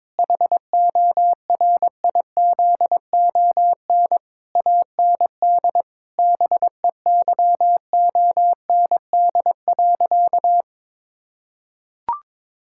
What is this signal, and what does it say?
Morse code, heard as a keyed tone, about 22 wpm, 700 Hz. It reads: HORIZON AND BEYOND.
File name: Difficult Sentences with Hints - 22wpm